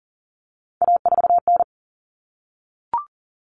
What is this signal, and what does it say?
Morse code, heard as a keyed tone, about 40 wpm, 700 Hz. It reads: A4N
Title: Common Cut Numbers - 3-Digit 40wpm